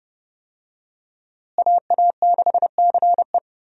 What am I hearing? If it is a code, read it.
Morse code, heard as a keyed tone, about 32 wpm, 700 Hz. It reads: AA6CE
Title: USA Call Signs - 5 Characters - Repeated 3x - 30wpm